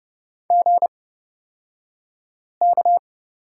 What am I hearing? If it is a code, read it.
Morse code, heard as a keyed tone, about 30 wpm, spaced as for 5 wpm, 700 Hz. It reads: GK
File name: Lesson 37d - Letter x - Two Characters 30wpm